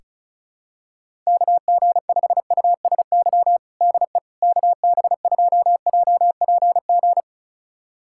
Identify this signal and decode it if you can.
Morse code, heard as a keyed tone, about 36 wpm, 700 Hz. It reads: KG5USY DE KB2JPG
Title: USA Callsign DE Callsign - 6 Characters - 35wpm